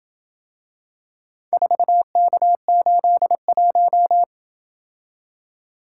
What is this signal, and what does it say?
Morse code, heard as a keyed tone, about 27 wpm, 700 Hz. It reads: 4K81